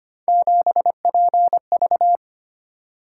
Morse code, heard as a keyed tone, about 25 wpm, 700 Hz. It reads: 7PV
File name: Three Letter-Number Combinations - 25wpm